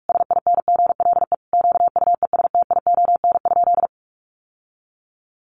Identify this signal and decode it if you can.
Morse code, heard as a keyed tone, about 44 wpm, 700 Hz. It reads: SINGLE QUESTION?
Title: Sentences from Top 700 Words 45wpm